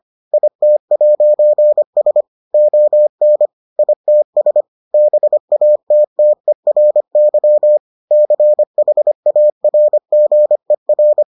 Morse code, heard as a keyed tone, about 25 wpm, 600 Hz. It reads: IT'S ON ITS BATTERY CHARGER